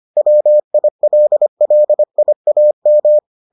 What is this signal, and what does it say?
Morse code, heard as a keyed tone, about 25 wpm, 600 Hz. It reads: WILLIAM